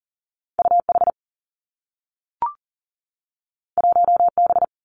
Morse code, heard as a keyed tone, about 42 wpm, 700 Hz. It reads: U H 1 B
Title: Two Challenging Letter-Number Combinations 40wpm